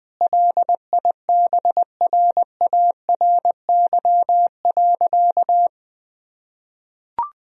Morse code, heard as a keyed tone, about 20 wpm, 700 Hz. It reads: LIBRARY.
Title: Sentences from Top 1000 Words - Review Part 4 - 20wpm